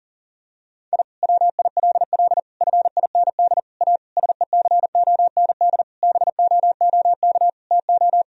Morse code, heard as a keyed tone, about 40 wpm, 700 Hz. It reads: I WILL FIND A SECOND BOOK TO
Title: Sentences from Top 1000 Words - Review Part 1 - 40wpm